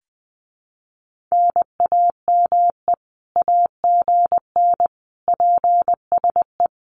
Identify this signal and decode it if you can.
Morse code, heard as a keyed tone, about 20 wpm, 700 Hz. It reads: NAME AGN PSE
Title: Common QSO Phrases Encore 20wpm